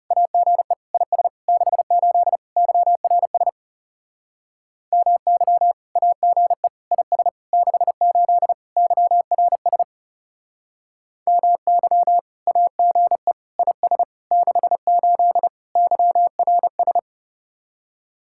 Morse code, heard as a keyed tone, about 37 wpm, 700 Hz. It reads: AGE IS 68 YRS MY AGE IS 68 YRS MY AGE IS 68 YRS
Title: Speed-Racing Common QSO Phrases - 30wpm